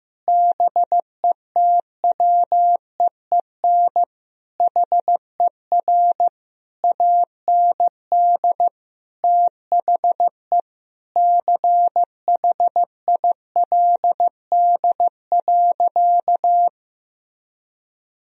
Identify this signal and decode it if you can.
Morse code, heard as a keyed tone, about 15 wpm, 700 Hz. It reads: BETWEEN HER AND THE CHILD.
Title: Sentences from Top 1000 Words - Review Part 1 - 15wpm